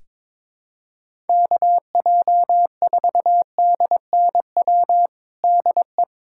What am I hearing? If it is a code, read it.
Morse code, heard as a keyed tone, about 22 wpm, 700 Hz. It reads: KJ4DNW DE
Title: USA Callsign DE Callsign - 6 Characters - 22wpm